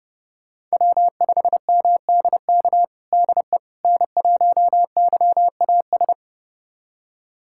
Morse code, heard as a keyed tone, about 31 wpm, 700 Hz. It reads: W5MDK DE N1YAS
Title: USA Callsign DE Callsign - 5 Characters - 30wpm